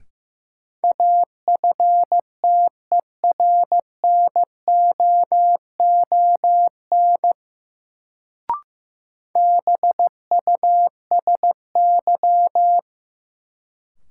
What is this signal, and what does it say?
Morse code, heard as a keyed tone, about 15 wpm, 700 Hz. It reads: AFTERNOON BUSY
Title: Sentences from Top 1500 Words - New Word Drill Alt - 15wpm